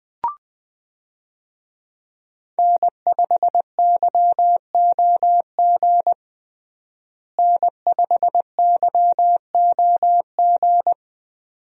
Morse code, heard as a keyed tone, about 20 wpm, 700 Hz. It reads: N5YOG N5YOG
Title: USA Call Signs - 5 Characters - Repeated 2x - 20wpm